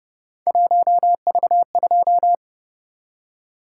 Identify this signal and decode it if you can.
Morse code, heard as a keyed tone, about 28 wpm, 700 Hz. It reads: 1V2